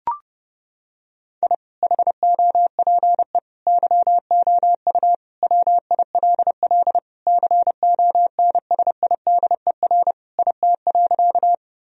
Morse code, heard as a keyed tone, about 32 wpm, 700 Hz. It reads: I HOPE YOU WILL CONSIDER IT.
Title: Sentences from Top 1000 Words - Review Part 4 - 30wpm